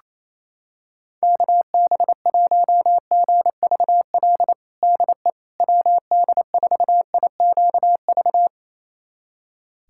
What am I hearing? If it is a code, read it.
Morse code, heard as a keyed tone, about 28 wpm, 700 Hz. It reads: KB1GVL DE WD4IQV
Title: USA Callsign DE Callsign - 6 Characters - 28wpm